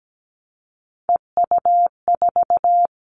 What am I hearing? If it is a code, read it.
Morse code, heard as a keyed tone, about 17 wpm, 700 Hz. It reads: EU4
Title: Callsign Prefixes for Top 100 DXCC - 17wpm